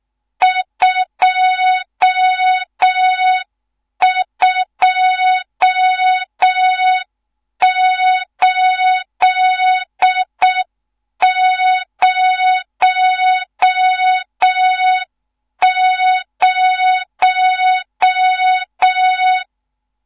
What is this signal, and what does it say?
Morse code, heard as a keyed tone, about 6 wpm, 3700 Hz, with noise. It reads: 22800
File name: Morze-HA2MN / hang / 14 280 HA2MN